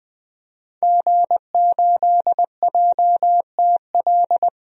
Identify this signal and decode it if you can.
Morse code, heard as a keyed tone, about 21 wpm, 700 Hz. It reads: G8JTL
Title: DXCC-100 Call Signs - 5 Characters - 20wpm